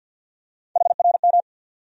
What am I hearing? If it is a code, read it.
Morse code, heard as a keyed tone, about 49 wpm, 700 Hz. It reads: HRM
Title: USA Call Sign Postfixes - 3 Letters 50wpm